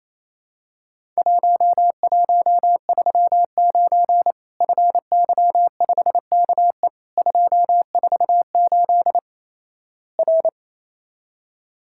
Morse code, heard as a keyed tone, about 27 wpm, 700 Hz. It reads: 1139 FY5KE 248 R
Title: Worked All Europe DX Contest - QTC Exchange - 28wpm